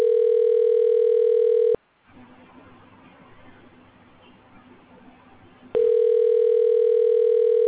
Ringback tone (North America), 440 + 480 Hz.